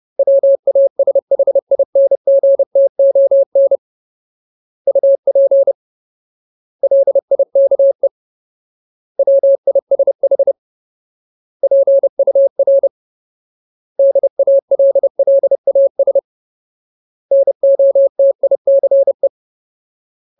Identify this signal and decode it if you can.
Morse code, heard as a keyed tone, about 30 wpm, 550 Hz. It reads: WASHINGTON UP LIKE WISH PUR DALLAS NOTICE
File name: Ultimate Word List / 30WPM Ultimate